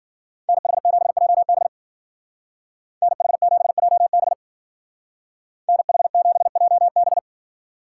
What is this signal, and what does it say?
Morse code, heard as a keyed tone, about 49 wpm, 700 Hz. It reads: NH7JB NH7JB NH7JB
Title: Speed Racing - USA 2x2 Call Signs 40wpm